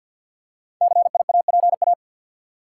Morse code, heard as a keyed tone, about 51 wpm, 700 Hz. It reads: CIAPA
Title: Code Groups - 5 Letters - 50wpm